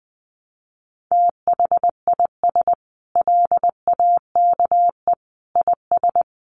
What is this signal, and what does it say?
Morse code, heard as a keyed tone, about 20 wpm, 700 Hz. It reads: THIS LAKE IS
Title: Sentences from Top 1100 Words 20wpm